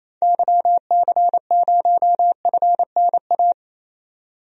Morse code, heard as a keyed tone, about 28 wpm, 700 Hz. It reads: YC0FNA
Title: DXCC-100 Call Signs - 6 Characters - 28wpm